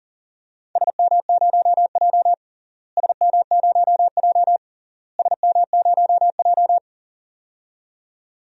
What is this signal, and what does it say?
Morse code, heard as a keyed tone, about 40 wpm, 700 Hz. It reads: SM0J SM0J SM0J
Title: DXCC-100 Call Signs - 4 Characters - Repeated 3x - 40wpm